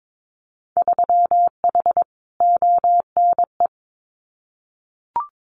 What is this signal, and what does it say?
Morse code, heard as a keyed tone, about 22 wpm, 700 Hz. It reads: 3H ONE
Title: Winter Field Day Exchanges Encore 22wpm